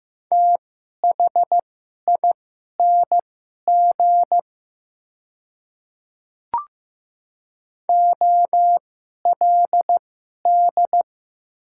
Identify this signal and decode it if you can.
Morse code, heard as a keyed tone, about 15 wpm, spaced as for 11 wpm, 700 Hz. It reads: THING OLD
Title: 2x Character Spacing - Top 300 Words - 15wpm